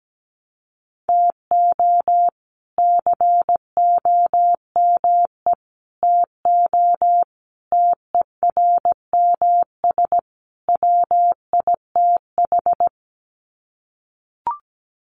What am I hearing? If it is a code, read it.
Morse code, heard as a keyed tone, about 17 wpm, 700 Hz. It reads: TO COME TO TERMS WITH
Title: Sets of 5 Words Encore 17wpm